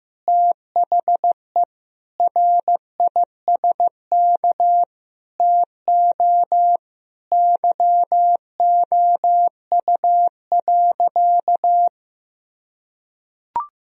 Morse code, heard as a keyed tone, about 15 wpm, 700 Hz. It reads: THE RISK TO YOU.
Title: Sentences from Top 1000 Words - Review Part 4 - 15wpm